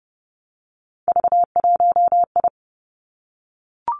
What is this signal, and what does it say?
Morse code, heard as a keyed tone, about 30 wpm, 700 Hz. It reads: V1I
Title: Three Challenging Letter-Number Combinations 30wpm